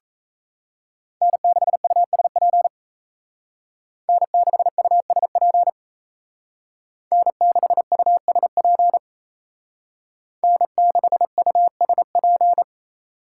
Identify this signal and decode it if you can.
Morse code, heard as a keyed tone, about 34 wpm, 700 Hz. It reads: N6USP N6USP N6USP N6USP
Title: Speed Racing - USA 1x3 Call Signs 28wpm